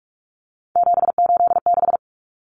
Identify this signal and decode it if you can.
Morse code, heard as a keyed tone, about 44 wpm, 700 Hz. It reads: 786
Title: Common Cut Numbers - 3-Digit 45wpm